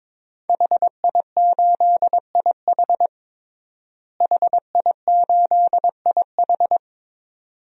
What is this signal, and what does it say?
Morse code, heard as a keyed tone, about 22 wpm, 700 Hz. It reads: HI8IH HI8IH
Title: DXCC-100 Call Signs - 5 Characters - Repeated 2x - 22wpm